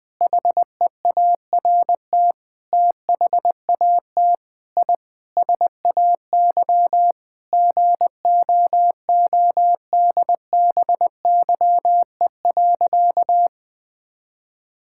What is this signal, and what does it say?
Morse code, heard as a keyed tone, about 20 wpm, 700 Hz. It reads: HEART THAT I SAY GOODBYE.
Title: Sentences with 7-Word Phrases - 20wpm